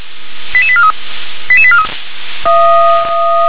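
Five-tone selective call (ZVEI), heard as quick runs of five tones, tones of about 70 ms. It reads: 88843 88843
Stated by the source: ZVEI five-tone selective call received by SDR.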